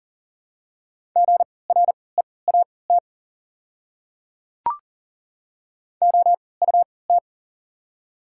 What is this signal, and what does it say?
Morse code, heard as a keyed tone, about 40 wpm, spaced as for 23 wpm, 690 Hz. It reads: GREAT OUT
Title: 3x Character Spacing - Top 200 Words - 40wpm